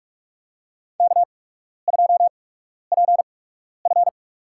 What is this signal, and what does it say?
Morse code, heard as a keyed tone, about 46 wpm, spaced as for 12 wpm, 690 Hz. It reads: K2PF
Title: USA Call Signs - 4 Characters - 8x Character Spacing - 45wpm